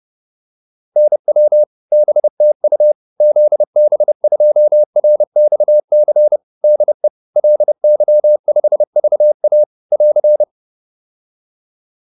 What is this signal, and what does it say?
Morse code, heard as a keyed tone, about 30 wpm, 600 Hz. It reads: NW BTU ZB2RXC DE LY5VA <AR>
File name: QSO406 30